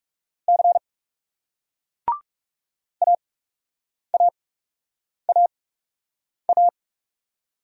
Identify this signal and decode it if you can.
Morse code, heard as a keyed tone, about 36 wpm, spaced as for 8 wpm, 700 Hz. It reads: C AAAA
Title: Speed-Racing Single Letters 30wpm